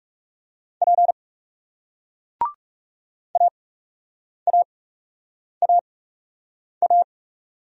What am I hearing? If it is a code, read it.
Morse code, heard as a keyed tone, about 41 wpm, spaced as for 8 wpm, 700 Hz. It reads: P AAAA